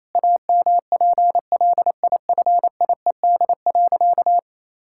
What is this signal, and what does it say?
Morse code, heard as a keyed tone, about 28 wpm, 700 Hz. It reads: AMPLIFIED.
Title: Pangram Challenge v2 - 28wpm